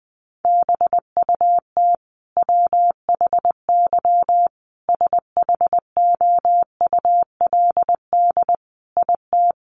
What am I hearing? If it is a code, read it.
Morse code, heard as a keyed tone, about 20 wpm, 700 Hz. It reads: BUT WHY SHOULD IT
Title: Sentences from Top 1200 Words 20wpm